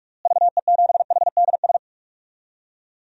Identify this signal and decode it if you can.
Morse code, heard as a keyed tone, about 46 wpm, 700 Hz. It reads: VE7HDS
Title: DXCC-100 Call Signs - 6 Characters - Repeated 2x - 45wpm